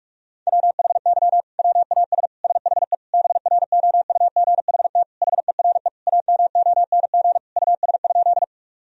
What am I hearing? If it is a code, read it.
Morse code, heard as a keyed tone, about 46 wpm, 700 Hz. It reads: WHY WAS SHE BROUGHT HERE AMONG US?
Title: Sentences from Top 1000 Words - Review Part 3 - 45wpm